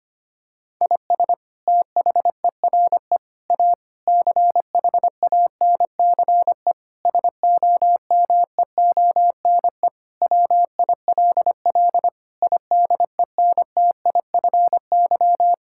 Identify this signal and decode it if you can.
Morse code, heard as a keyed tone, about 25 wpm, 700 Hz. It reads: IS THERE A CHANCE SOMEONE WILL IDENTIFY
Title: Sentences from Top 1000 Words - Review Part 6 - 25wpm